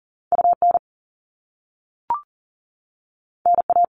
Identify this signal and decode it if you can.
Morse code, heard as a keyed tone, about 40 wpm, 700 Hz. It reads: UN NA
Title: All Cut Numbers - 2-Digit 40wpm